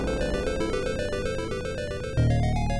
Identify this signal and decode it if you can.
There is no code.